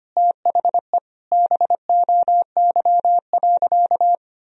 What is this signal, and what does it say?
Morse code, heard as a keyed tone, about 25 wpm, 700 Hz. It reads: THE BOY.
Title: Sentences from Top 1000 Words - Review Part 4 - 25wpm